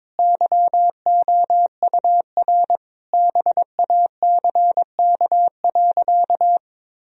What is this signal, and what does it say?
Morse code, heard as a keyed tone, about 21 wpm, 700 Hz. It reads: YOUR BACK.
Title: Sentences from Top 1000 Words - Review Part 1 - 22wpm